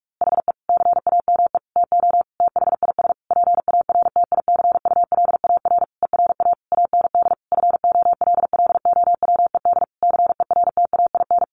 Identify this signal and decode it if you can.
Morse code, heard as a keyed tone, about 44 wpm, 700 Hz. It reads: HE CAME TO THIS PARTICULAR ERA AND FOLLOWED CERTAIN
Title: A Sound Of Thunder 45wpm